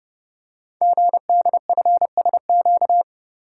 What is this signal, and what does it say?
Morse code, heard as a keyed tone, about 30 wpm, 690 Hz. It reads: GDFSQ
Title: Code Groups - 5 Letters - 30wpm